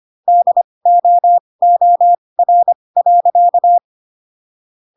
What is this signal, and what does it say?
Morse code, heard as a keyed tone, about 25 wpm, spaced as for 20 wpm, 700 Hz. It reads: DOOR.
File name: Pinocchio / Pinocchio ch6 15 25 ik1hge